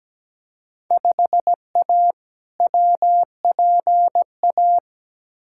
Morse code, heard as a keyed tone, about 17 wpm, 700 Hz. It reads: 5A WPA
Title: Field Day Exchanges Encore - 17wpm